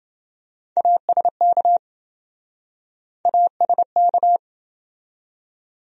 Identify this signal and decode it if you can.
Morse code, heard as a keyed tone, about 28 wpm, 700 Hz. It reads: ASK ASK